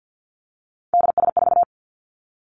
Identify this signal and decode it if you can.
Morse code, heard as a keyed tone, about 50 wpm, 700 Hz. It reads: DS4
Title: Three Challenging Letter-Number Combinations 50wpm